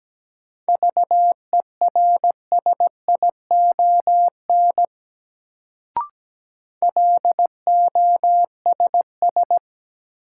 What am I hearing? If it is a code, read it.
Morse code, heard as a keyed tone, about 17 wpm, 700 Hz. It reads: VERSION LOSS
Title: Sentences from Top 1400 Words - New Word Drill - 17wpm